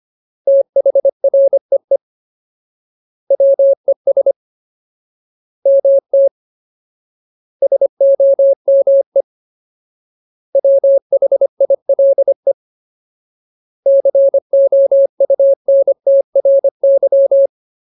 Morse code, heard as a keyed tone, about 25 wpm, 550 Hz. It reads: THREE WES MT SOME WHILE COUNTRY
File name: Ultimate Word List / 25WPM Ultimate